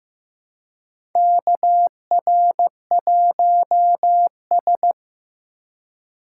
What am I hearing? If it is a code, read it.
Morse code, heard as a keyed tone, about 15 wpm, 700 Hz. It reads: KR1S
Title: USA Call Signs - 4 Characters - 2x Character Spacing - 15wpm